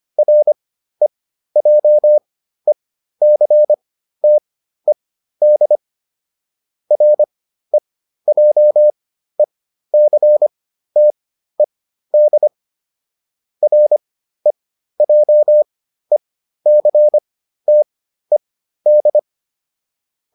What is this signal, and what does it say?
Morse code, heard as a keyed tone, about 25 wpm, spaced as for 13 wpm, 600 Hz. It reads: REJECTED REJECTED REJECTED
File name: RE8 13